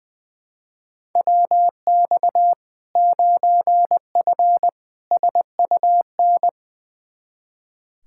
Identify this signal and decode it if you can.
Morse code, heard as a keyed tone, about 20 wpm, 700 Hz. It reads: WX 9F SUN
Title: Practice QSO - Exchange 2 - Info - 20wpm